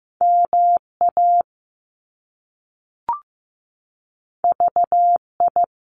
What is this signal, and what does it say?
Morse code, heard as a keyed tone, about 15 wpm, 700 Hz. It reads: MA VI
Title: Top Two Letter Combinations in Words 15wpm